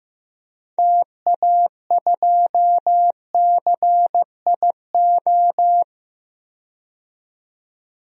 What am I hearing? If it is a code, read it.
Morse code, heard as a keyed tone, about 15 wpm, 700 Hz. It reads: TA2CIO
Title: DXCC-100 Call Signs - 6 Characters - 15wpm